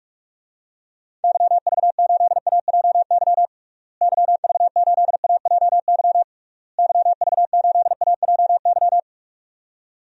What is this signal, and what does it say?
Morse code, heard as a keyed tone, about 46 wpm, 700 Hz. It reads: YV8AJY YV8AJY YV8AJY
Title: DXCC-100 Call Signs - 6 Characters - Repeated 3x - 45wpm